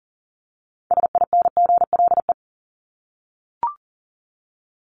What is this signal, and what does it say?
Morse code, heard as a keyed tone, about 40 wpm, 700 Hz. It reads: SINGLE
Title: Top 501-750 Words 40wpm